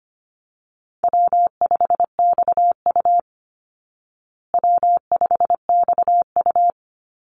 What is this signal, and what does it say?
Morse code, heard as a keyed tone, about 25 wpm, 700 Hz. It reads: W5XU W5XU
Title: Call Signs Repeated 2x - 25wpm